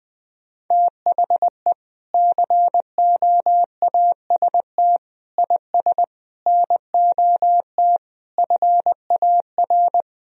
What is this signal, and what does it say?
Morse code, heard as a keyed tone, about 21 wpm, 700 Hz. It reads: THE COAST IS NOT FAR
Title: Sentences from Top 1000 Words - Review Part 4 - 20wpm